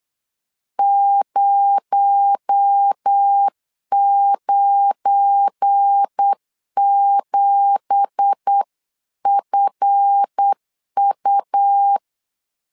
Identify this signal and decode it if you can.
Morse code, heard as a keyed tone, about 8 wpm, 800 Hz. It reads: 097FU
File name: CW / DET / 1 9 1500C S A 8wpm 5637